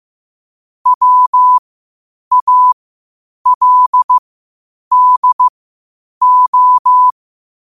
Morse code, heard as a keyed tone, about 15 wpm, spaced as for 8 wpm, 1000 Hz. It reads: WALDO
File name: qotdfiles / 8wpm